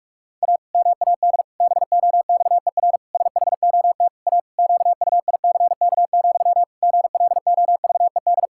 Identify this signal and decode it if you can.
Morse code, heard as a keyed tone, about 46 wpm, 700 Hz. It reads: A MAD BOXER SHOT A QUICK, GLOVED
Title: Pangram Challenge v2 - 45wpm